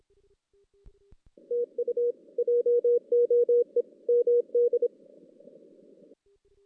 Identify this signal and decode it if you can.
Morse code, heard as a keyed tone, about 29 wpm, 470 Hz. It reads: TU JOE MD
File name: 2016 JAN CW NAQP